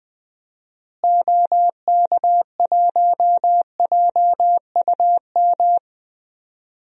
Morse code, heard as a keyed tone, about 21 wpm, 700 Hz. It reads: OK1JUM
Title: DXCC-100 Call Signs - 6 Characters - Repeated 2x - 20wpm